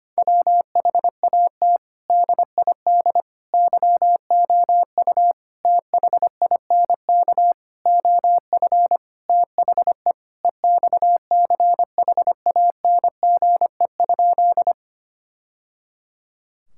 Morse code, heard as a keyed tone, about 25 wpm, 700 Hz. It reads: WHAT DID YOU THINK OF THE EXCHANGE?
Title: Sentences from Top 1000 Words - Review Part 6 - 25wpm